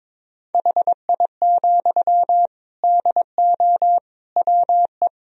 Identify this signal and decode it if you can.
Morse code, heard as a keyed tone, about 21 wpm, 700 Hz. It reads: HI, DO WE
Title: Sentences from Top 1000 Words - Review Part 3 - 22wpm